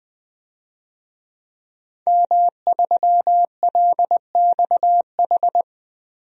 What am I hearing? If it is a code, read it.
Morse code, heard as a keyed tone, about 20 wpm, 700 Hz. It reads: M3LXH
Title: DXCC-100 Call Signs - 5 Characters - Repeated 2x - 20wpm